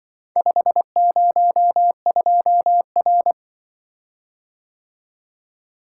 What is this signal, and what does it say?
Morse code, heard as a keyed tone, about 24 wpm, 700 Hz. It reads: 502R
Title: Speed-Racing - Four Letter-Number Combinations - 20wpm